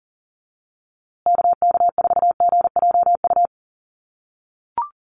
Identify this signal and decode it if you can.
Morse code, heard as a keyed tone, about 40 wpm, 700 Hz. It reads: KK4GJU
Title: USA Call Signs Repeated 3x 40wpm